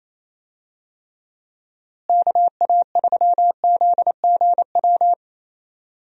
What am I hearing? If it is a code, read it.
Morse code, heard as a keyed tone, about 28 wpm, 700 Hz. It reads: KA3ZGW
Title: USA Call Signs - 6 Characters - Repeated 3x - 28wpm